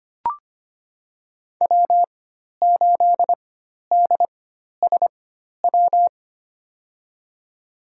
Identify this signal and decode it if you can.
Morse code, heard as a keyed tone, about 25 wpm, spaced as for 12 wpm, 700 Hz. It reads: W8DSW
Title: USA Call Signs - 4x Character Spacing - 25wpm